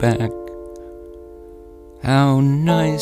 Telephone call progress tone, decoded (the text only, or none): dial tone